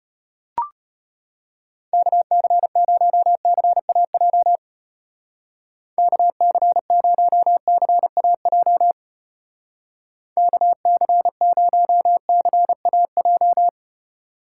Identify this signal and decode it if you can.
Morse code, heard as a keyed tone, about 31 wpm, 700 Hz. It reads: KC0CAJ KC0CAJ KC0CAJ
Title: Speed Racing - USA 2x3 Call Signs 25wpm